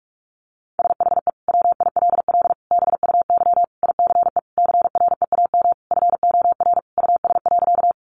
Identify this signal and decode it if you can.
Morse code, heard as a keyed tone, about 44 wpm, 700 Hz. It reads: SHE WILL BUY ICE CREAM FOR US.
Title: Sentences from Top 1500 Words 45wpm